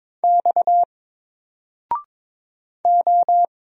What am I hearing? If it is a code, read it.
Morse code, heard as a keyed tone, about 22 wpm, 700 Hz. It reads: X O
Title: Single Letter-Number Alt 22wpm